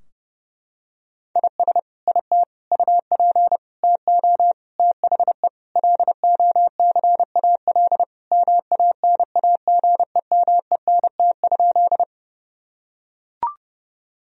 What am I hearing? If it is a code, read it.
Morse code, heard as a keyed tone, about 30 wpm, 700 Hz. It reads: IS IT UP TO THE LOCAL MANAGEMENT?
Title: Sentences from Top 1000 Words - Review Part 4 - 30wpm